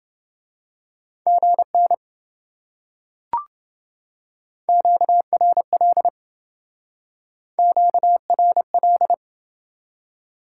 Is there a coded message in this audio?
Yes: Morse code from 1.3 to 9.1 s, heard as a keyed tone, about 27 wpm, 700 Hz. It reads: GN QRL QRL